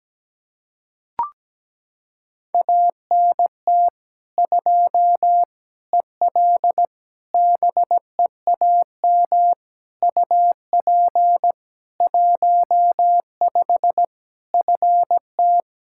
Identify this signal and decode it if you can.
Morse code, heard as a keyed tone, about 17 wpm, 700 Hz. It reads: ANT 2 EL BEAM UP 15 FT
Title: Practice QSO - Exchange 2 - Info - 17wpm